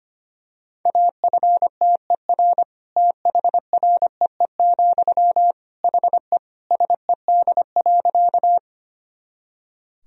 Morse code, heard as a keyed tone, about 25 wpm, 700 Hz. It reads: AFTER THREE, HE SED.
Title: Sentences from Top 200 Words and Abbreviations 25wpm